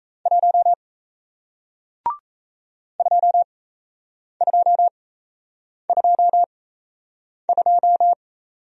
Morse code, heard as a keyed tone, about 38 wpm, spaced as for 8 wpm, 700 Hz. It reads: 1 2222